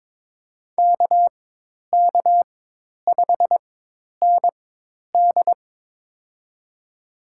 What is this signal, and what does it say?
Morse code, heard as a keyed tone, about 22 wpm, spaced as for 10 wpm, 700 Hz. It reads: KK5ND
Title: USA Call Signs - 5 Characters - 4x Character Spacing - 22wpm